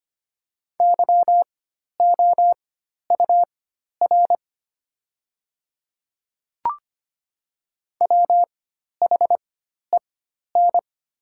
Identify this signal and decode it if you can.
Morse code, heard as a keyed tone, about 25 wpm, spaced as for 12 wpm, 700 Hz. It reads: YOUR WHEN
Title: 4x Character Spacing - Top 200 Words - 25wpm